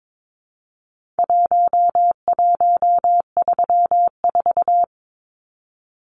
Morse code, heard as a keyed tone, about 22 wpm, 700 Hz. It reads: 1134
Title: Numbers - Contest Style - 4-Digit 22wpm